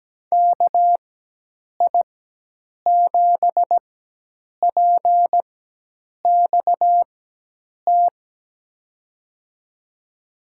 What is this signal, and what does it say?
Morse code, heard as a keyed tone, about 17 wpm, spaced as for 8 wpm, 700 Hz. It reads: KI7PXT
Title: USA Call Signs - 6 Characters - 4x Character Spacing - 17wpm